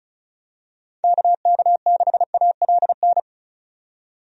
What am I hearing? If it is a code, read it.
Morse code, heard as a keyed tone, about 36 wpm, 700 Hz. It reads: KK6ALN
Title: USA Call Signs - 6 Characters - 35wpm